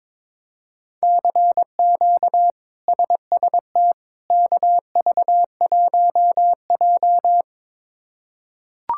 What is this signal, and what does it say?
Morse code, heard as a keyed tone, about 22 wpm, 700 Hz. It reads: CQ SST KV1J
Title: K1USN SST QSO Practice - 22wpm